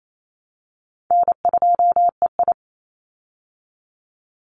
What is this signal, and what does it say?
Morse code, heard as a keyed tone, about 28 wpm, 700 Hz. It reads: N2EI
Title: Call Signs Repeated 2x - 28wpm